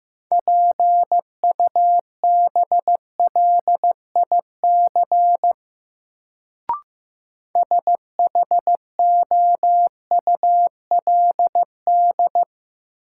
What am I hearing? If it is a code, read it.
Morse code, heard as a keyed tone, about 15 wpm, 700 Hz. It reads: PUBLIC SHOULD
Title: Sets of 2 Words - New Word Drill - 15wpm